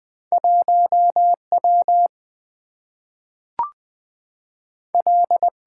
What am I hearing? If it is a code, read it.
Morse code, heard as a keyed tone, about 20 wpm, 700 Hz. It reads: 1 W L
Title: Two Letter-Number Combinations - 20wpm